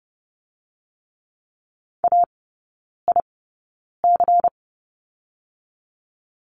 Morse code, heard as a keyed tone, about 30 wpm, spaced as for 9 wpm, 700 Hz. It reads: AIC
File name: Lesson 16e - Letter c - Three Characters 30wpm